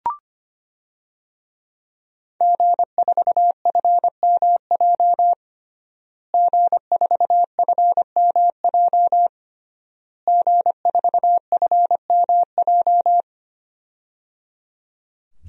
Morse code, heard as a keyed tone, about 25 wpm, 700 Hz. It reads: G4FMJ G4FMJ G4FMJ
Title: DXCC-100 Call Signs - 5 Characters - Repeated 3x - 25wpm